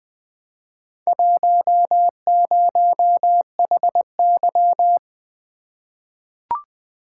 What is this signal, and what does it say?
Morse code, heard as a keyed tone, about 20 wpm, 700 Hz. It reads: 10HY
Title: Four Letter-Number Combinations - 20wpm